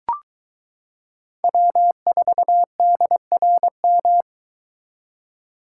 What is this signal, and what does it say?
Morse code, heard as a keyed tone, about 22 wpm, 700 Hz. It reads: W4DRM